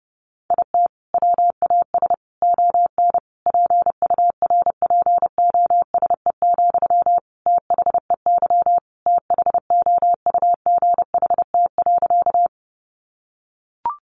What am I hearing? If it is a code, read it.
Morse code, heard as a keyed tone, about 30 wpm, 700 Hz. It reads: IT WAS ON PURPOSE, THEY THOUGHT.
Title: Sentences from Top 800 Words 30wpm